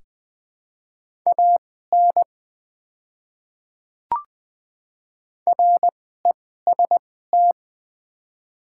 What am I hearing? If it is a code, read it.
Morse code, heard as a keyed tone, about 20 wpm, spaced as for 15 wpm, 690 Hz. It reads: AN REST
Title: 2x Character Spacing - Top 500 Words - 20wpm